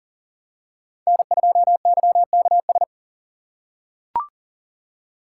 Morse code, heard as a keyed tone, about 40 wpm, 700 Hz. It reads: N2YKS
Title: USA Call Signs - 5 Characters - 3x Character Spacing - 40wpm